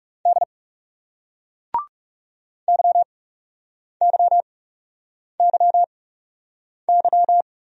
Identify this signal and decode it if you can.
Morse code, heard as a keyed tone, about 41 wpm, spaced as for 8 wpm, 700 Hz. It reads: D YYYY